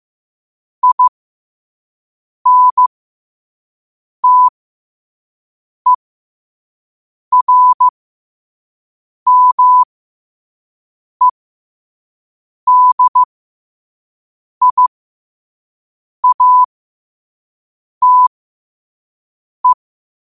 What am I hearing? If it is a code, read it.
Morse code, heard as a keyed tone, about 15 wpm, spaced as for 5 wpm, 1000 Hz. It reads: INTERMEDIATE
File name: qotdfiles / 5wpm